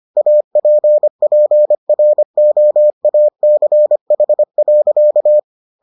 Morse code, heard as a keyed tone, about 25 wpm, 600 Hz. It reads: APPROACH.